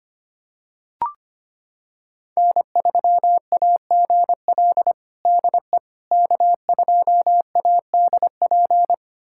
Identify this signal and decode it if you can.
Morse code, heard as a keyed tone, about 25 wpm, 700 Hz. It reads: N3AGL DE K2ADP
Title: USA Callsign DE Callsign - 5 Characters - 25wpm